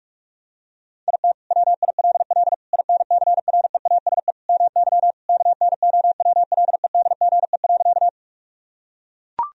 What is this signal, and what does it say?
Morse code, heard as a keyed tone, about 44 wpm, 700 Hz. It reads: IT WILL INCREASE MY KNOWLEDGE.
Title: Sentences from Top 1000 Words - Review Part 4 - 45wpm